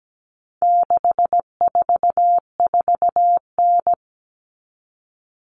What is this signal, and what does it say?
Morse code, heard as a keyed tone, about 17 wpm, 700 Hz. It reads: 644N
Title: All Cut Numbers - 4-Digit 17wpm